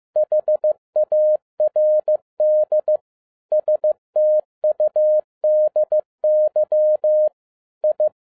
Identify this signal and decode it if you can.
Morse code, heard as a keyed tone, about 15 wpm, 610 Hz. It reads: HARD STUDY I
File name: 2011-07-mp3 / n07 A LOT HAS CHANGED 15wpm